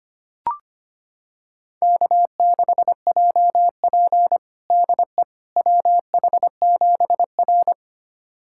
Morse code, heard as a keyed tone, about 25 wpm, 700 Hz. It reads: K6JP DE WH7R
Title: USA Callsign DE Callsign - 4 Characters - 25wpm